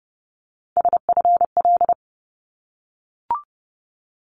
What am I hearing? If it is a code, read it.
Morse code, heard as a keyed tone, about 30 wpm, 700 Hz. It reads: SFL
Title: ARRL Sections 30wpm